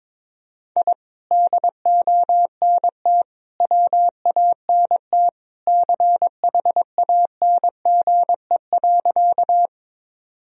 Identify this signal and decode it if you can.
Morse code, heard as a keyed tone, about 22 wpm, 700 Hz. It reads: I DONT WANT CHANGE.
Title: Sentences from Top 1000 Words - Review Part 1 - 22wpm